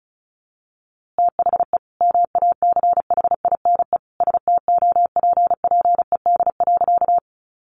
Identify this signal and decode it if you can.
Morse code, heard as a keyed tone, about 34 wpm, 700 Hz. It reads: THE MACHINE STOPPED.
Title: A Sound Of Thunder 35wpm